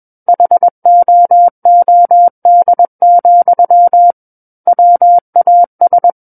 Morse code, heard as a keyed tone, about 21 wpm, 700 Hz. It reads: HOOD, WAS
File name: Book One / 12-Chapter 12 - 21 WPM